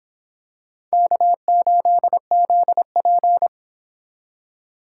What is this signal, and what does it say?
Morse code, heard as a keyed tone, about 26 wpm, 700 Hz. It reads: K8ZP